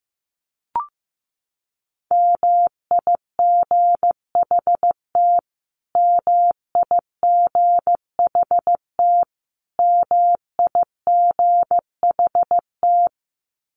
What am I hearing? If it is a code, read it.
Morse code, heard as a keyed tone, about 15 wpm, 700 Hz. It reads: MIGHT MIGHT MIGHT
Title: Top 300 Words Repeated 15wpm